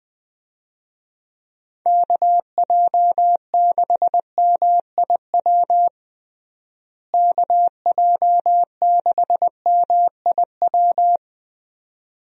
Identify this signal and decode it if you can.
Morse code, heard as a keyed tone, about 20 wpm, 700 Hz. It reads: KJ6MIW KJ6MIW
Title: USA Call Signs - 6 Characters - Repeated 2x - 20wpm